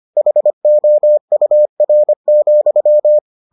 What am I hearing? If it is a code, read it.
Morse code, heard as a keyed tone, about 25 wpm, 600 Hz. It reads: HOUR,